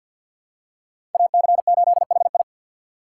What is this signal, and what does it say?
Morse code, heard as a keyed tone, about 47 wpm, 700 Hz. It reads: AC8HI
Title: USA Call Signs - 5 Characters - 4x Character Spacing - 50wpm